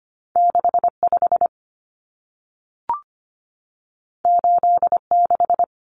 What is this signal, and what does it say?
Morse code, heard as a keyed tone, about 25 wpm, 700 Hz. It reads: 65 86
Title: Common Cut Numbers - 2-Digit 25wpm